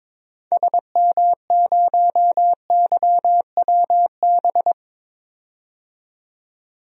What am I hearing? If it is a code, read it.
Morse code, heard as a keyed tone, about 22 wpm, 700 Hz. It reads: SM0YWB
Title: DXCC-100 Call Signs - 6 Characters - 22wpm